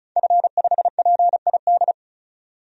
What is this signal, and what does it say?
Morse code, heard as a keyed tone, about 35 wpm, 700 Hz. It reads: F5PID
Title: DXCC-100 Call Signs - 5 Characters - 35wpm